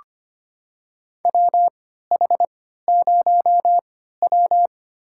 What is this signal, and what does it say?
Morse code, heard as a keyed tone, about 25 wpm, spaced as for 14 wpm, 700 Hz. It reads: WH0W
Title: USA Call Signs - 4 Characters - 3x Character Spacing - 25wpm